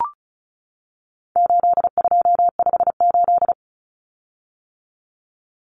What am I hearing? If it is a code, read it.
Morse code, heard as a keyed tone, about 34 wpm, 700 Hz. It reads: 8258
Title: Common Cut Numbers - 4-Digit 35wpm